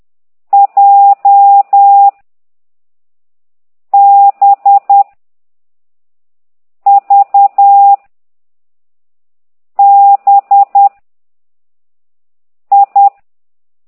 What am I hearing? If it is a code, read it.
Morse code, heard as a keyed tone, about 10 wpm, 800 Hz. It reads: J B V B I
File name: ct2idw / exerc6